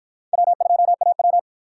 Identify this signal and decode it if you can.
Morse code, heard as a keyed tone, about 50 wpm, 700 Hz. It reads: W2AW